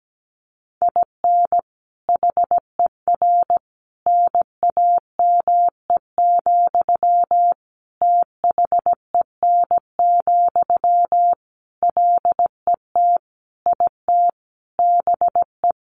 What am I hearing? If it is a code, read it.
Morse code, heard as a keyed tone, about 17 wpm, 700 Hz. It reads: IN HER NAME, THEN, LET IT BE
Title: Sentences from Top 500 Words 17wpm